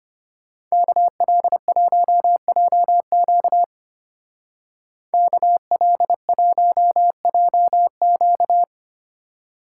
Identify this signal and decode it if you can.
Morse code, heard as a keyed tone, about 31 wpm, 700 Hz. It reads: KL1JQ KL1JQ
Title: Speed Racing - USA 2x2 Call Signs 25wpm